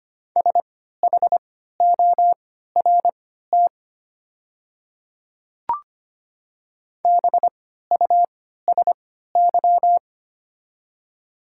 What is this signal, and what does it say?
Morse code, heard as a keyed tone, about 25 wpm, spaced as for 14 wpm, 700 Hz. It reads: SHORT BUSY
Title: 3x Character Spacing - Top 500 Words - 25wpm